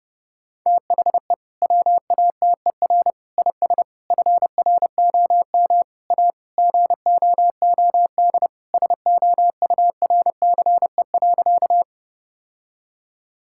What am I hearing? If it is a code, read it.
Morse code, heard as a keyed tone, about 30 wpm, 700 Hz. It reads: THE WATER IS FROM A GOOD SOURCE.
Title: Sentences from Top 1000 Words - Review Part 5 - 30wpm